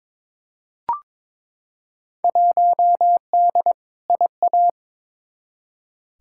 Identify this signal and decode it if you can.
Morse code, heard as a keyed tone, about 22 wpm, 700 Hz. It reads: 1D IA
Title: Field Day Exchanges - 22wpm